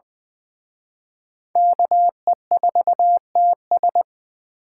Morse code, heard as a keyed tone, about 24 wpm, 700 Hz. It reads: KE4TS
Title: Speed Racing - USA 2x2 Call Signs 20wpm